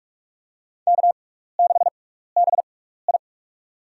Morse code, heard as a keyed tone, about 44 wpm, spaced as for 15 wpm, 700 Hz. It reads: K6BI
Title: USA Call Signs - 4 Characters - 6x Character Spacing - 45wpm